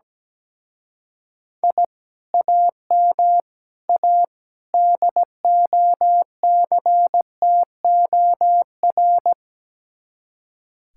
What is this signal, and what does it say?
Morse code, heard as a keyed tone, about 17 wpm, 700 Hz. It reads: I AM A DOCTOR
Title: Speed-Racing Common QSO Phrases - 17wpm